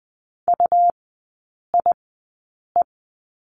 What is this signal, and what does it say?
Morse code, heard as a keyed tone, about 20 wpm, spaced as for 8 wpm, 690 Hz. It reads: UIE
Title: 145 Lesson 24e - Letter y - Three Characters 20wpm